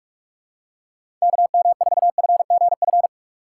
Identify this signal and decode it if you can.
Morse code, heard as a keyed tone, about 46 wpm, 700 Hz. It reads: KM4FGF
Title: USA Call Signs - 6 Characters - 2x Character Spacing - 45wpm